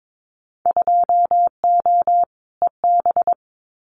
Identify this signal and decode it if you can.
Morse code, heard as a keyed tone, about 22 wpm, 700 Hz. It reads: 2O EB
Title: Winter Field Day Exchanges Encore 22wpm